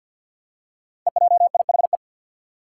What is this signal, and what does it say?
Morse code, heard as a keyed tone, about 52 wpm, 700 Hz. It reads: EJIHE
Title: Five Challenging Letter-Number Combinations 50wpm